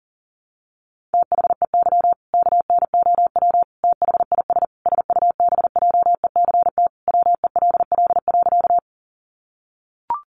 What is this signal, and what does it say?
Morse code, heard as a keyed tone, about 40 wpm, 700 Hz. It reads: THEY KNOW THIS SUBJECT WELL.
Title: Sentences from Top 800 Words 40wpm